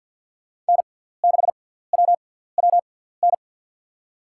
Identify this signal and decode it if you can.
Morse code, heard as a keyed tone, about 48 wpm, spaced as for 17 wpm, 700 Hz. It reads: N6WWN
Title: USA Call Signs - 5 Characters - 6x Character Spacing - 50wpm